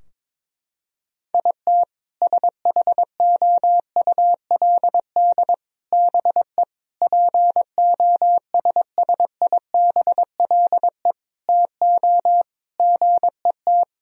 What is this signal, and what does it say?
Morse code, heard as a keyed tone, about 21 wpm, 700 Hz. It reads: IT SHOULD BE POSSIBLE TO GET
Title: Sentences from Top 1000 Words - Review Part 4 - 22wpm